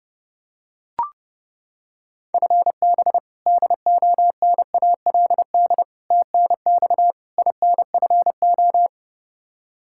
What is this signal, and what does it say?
Morse code, heard as a keyed tone, about 32 wpm, 700 Hz. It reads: FB DONALD TNX INFO
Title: Practice QSO - Exchange 3 - More Info - 30wpm